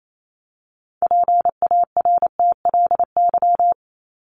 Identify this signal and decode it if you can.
Morse code, heard as a keyed tone, about 28 wpm, 700 Hz. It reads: PARTLY
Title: LY Suffix - Top 100 Words with Base Words 28wpm